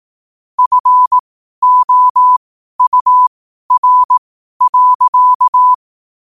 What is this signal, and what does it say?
Morse code, heard as a keyed tone, about 18 wpm, spaced as for 12 wpm, 1000 Hz. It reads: FOUR.
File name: qotdfiles / 12wpm